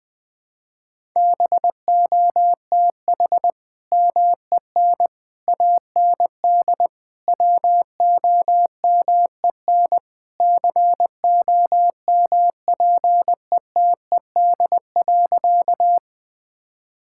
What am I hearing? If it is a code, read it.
Morse code, heard as a keyed tone, about 21 wpm, 700 Hz. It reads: BOTH MEN AND WOMEN COMPETED.
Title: Sentences with 3-Word Phrases - 20wpm